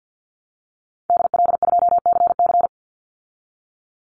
Morse code, heard as a keyed tone, about 50 wpm, 700 Hz. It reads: DL2CC
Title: Call Signs Repeated 2x - 50wpm